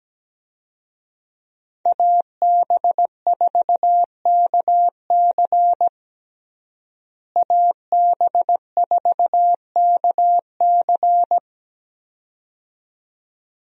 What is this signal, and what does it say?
Morse code, heard as a keyed tone, about 17 wpm, 700 Hz. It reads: AB4KC AB4KC
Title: USA Call Signs - 5 Characters - Repeated 2x - 17wpm